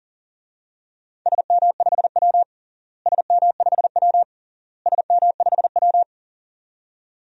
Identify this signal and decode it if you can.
Morse code, heard as a keyed tone, about 40 wpm, 700 Hz. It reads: SM5W SM5W SM5W
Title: DXCC-100 Call Signs - 4 Characters - Repeated 3x - 40wpm